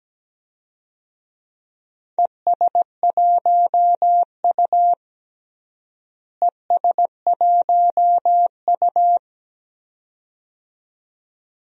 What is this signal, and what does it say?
Morse code, heard as a keyed tone, about 17 wpm, 700 Hz. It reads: ES1U ES1U
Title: DXCC-100 Call Signs - 4 Characters - Repeated 2x - 17wpm